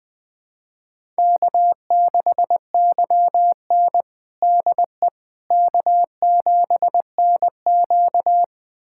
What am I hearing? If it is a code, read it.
Morse code, heard as a keyed tone, about 20 wpm, 700 Hz. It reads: K6YN DE K7NQ
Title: USA Callsign DE Callsign - 4 Characters - 20wpm